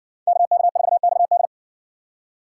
Morse code, heard as a keyed tone, about 60 wpm, 700 Hz. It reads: KK4XD